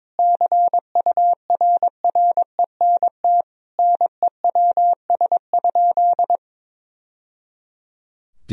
Morse code, heard as a keyed tone, about 22 wpm, 700 Hz. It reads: CURRENT NEWS?
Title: Sentences from Top 1000 Words - Review Part 4 - 22wpm